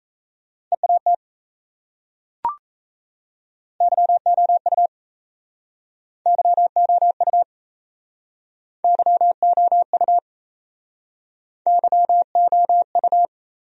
Morse code, heard as a keyed tone, about 36 wpm, 700 Hz. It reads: EAT YOU YOU YOU YOU